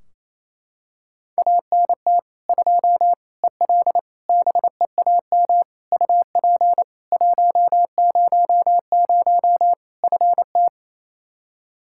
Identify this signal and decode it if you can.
Morse code, heard as a keyed tone, about 28 wpm, 700 Hz. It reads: ANT 2 EL BEAM UP 100 FT
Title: Practice QSO - Exchange 2 - Info - 28wpm